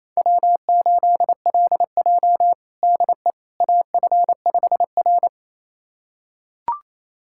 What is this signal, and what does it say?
Morse code, heard as a keyed tone, about 28 wpm, 700 Hz. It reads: W8LJ DE AF5R
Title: USA Callsign DE Callsign - 4 Characters - 28wpm